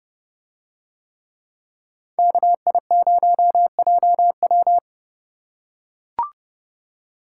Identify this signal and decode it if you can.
Morse code, heard as a keyed tone, about 28 wpm, 700 Hz. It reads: KI0JW